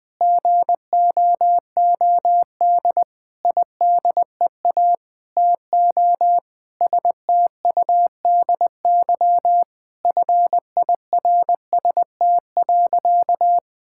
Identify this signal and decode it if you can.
Morse code, heard as a keyed tone, about 20 wpm, 700 Hz. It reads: GOOD IDEA TO STUDY FIRST.
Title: Sentences with 7-Word Phrases - 20wpm